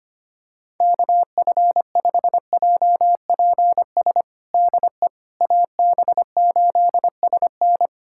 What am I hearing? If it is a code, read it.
Morse code, heard as a keyed tone, about 25 wpm, 700 Hz. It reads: KF5JPS DE AB8SN
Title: USA Callsign DE Callsign 25wpm